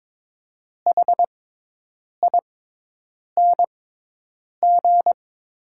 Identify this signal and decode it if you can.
Morse code, heard as a keyed tone, about 22 wpm, spaced as for 8 wpm, 700 Hz. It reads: HING
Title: 6x Character Spacing - Top 500 Words - 22wpm